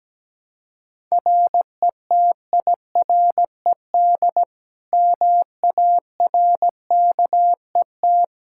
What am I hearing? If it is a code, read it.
Morse code, heard as a keyed tone, about 17 wpm, 700 Hz. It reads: RETIRED MARKET
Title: Practice QSO - Exchange 3 - More Info - 17wpm